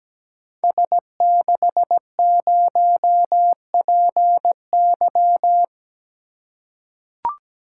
Morse code, heard as a keyed tone, about 17 wpm, 700 Hz. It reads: S60PY
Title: Code Groups - 5 Letter:Numbers - 17wpm